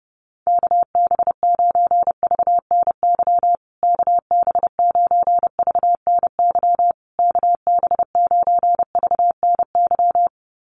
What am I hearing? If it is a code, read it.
Morse code, heard as a keyed tone, about 30 wpm, 700 Hz. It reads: KB9VNY KB9VNY KB9VNY
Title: USA Call Signs Repeated 3x 30wpm